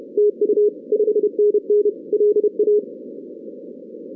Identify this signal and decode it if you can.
Morse code, heard as a keyed tone, about 29 wpm, 440 Hz. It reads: TU 5NN LA
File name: 2016 arrl dx cw / 2016-ARRL-DX-CW-K5TR- - 0-17